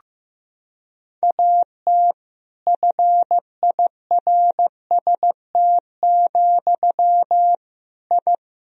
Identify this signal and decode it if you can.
Morse code, heard as a keyed tone, about 16 wpm, 700 Hz. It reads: AT FIRST, I
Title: Sentences from Top 1000 Words - Review Part 4 - 15wpm